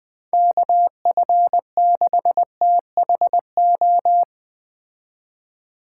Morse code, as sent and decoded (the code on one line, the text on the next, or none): -.- ..-. -.... - .... ---
KF6THO